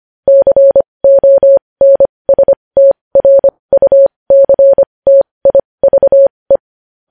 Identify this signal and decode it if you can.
Morse code, heard as a keyed tone, about 25 wpm, spaced as for 20 wpm, 550 Hz. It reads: CONSTRUCTIVE
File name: podcast / News headlines-1x-25x20